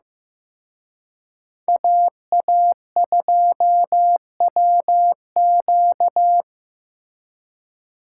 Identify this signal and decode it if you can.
Morse code, heard as a keyed tone, about 18 wpm, 700 Hz. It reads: AA2WQ
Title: Speed Racing - USA 2x2 Call Signs 15wpm